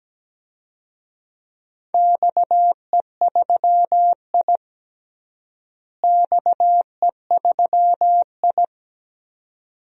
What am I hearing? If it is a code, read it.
Morse code, heard as a keyed tone, about 17 wpm, 700 Hz. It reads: XE3I XE3I
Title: DXCC-100 Call Signs - 4 Characters - Repeated 2x - 17wpm